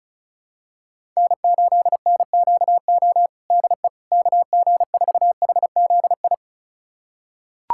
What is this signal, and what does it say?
Morse code, heard as a keyed tone, about 34 wpm, 700 Hz. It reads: N8NQO DE KG4HZI
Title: USA Callsign DE Callsign 35wpm